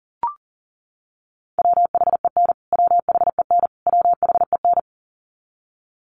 Morse code, heard as a keyed tone, about 40 wpm, 700 Hz. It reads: WHEN WHEN WHEN
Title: Top 300 Words Repeated 40wpm